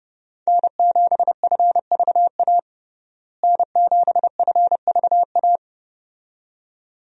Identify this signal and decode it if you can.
Morse code, heard as a keyed tone, about 31 wpm, 700 Hz. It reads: N7FVA N7FVA
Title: USA Call Signs - 5 Characters - Repeated 2x - 30wpm